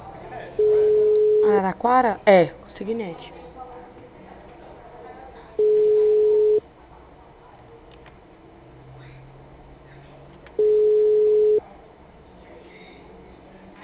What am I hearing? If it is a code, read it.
Ringback tone (Europe, 425 Hz), with speech.